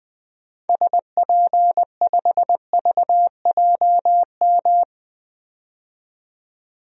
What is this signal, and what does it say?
Morse code, heard as a keyed tone, about 21 wpm, 700 Hz. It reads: SP5VJM
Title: DXCC-100 Call Signs - 6 Characters - 20wpm